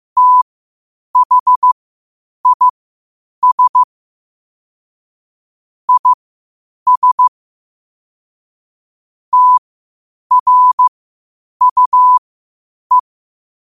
Morse code, heard as a keyed tone, about 15 wpm, spaced as for 8 wpm, 1000 Hz. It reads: THIS IS TRUE
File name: qotdfiles / 8wpm